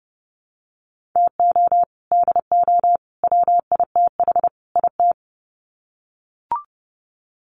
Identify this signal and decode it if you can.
Morse code, heard as a keyed tone, about 30 wpm, 700 Hz. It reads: TO DO WITH IT
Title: Sets of 4 Words 30wpm